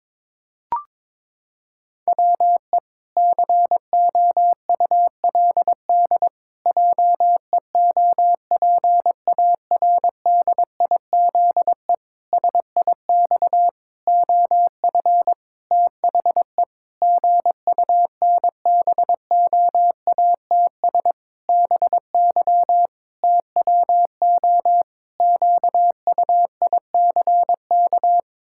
Morse code, heard as a keyed tone, about 22 wpm, 700 Hz. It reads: WE COULD JEOPARDIZE SIX OF THE GUNBOATS BY TWO QUICK
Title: Pangram Challenge v2 - 22wpm